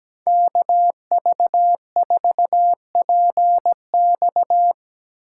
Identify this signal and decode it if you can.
Morse code, heard as a keyed tone, about 17 wpm, 700 Hz. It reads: KV4PX
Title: Speed Racing - USA 2x2 Call Signs 17wpm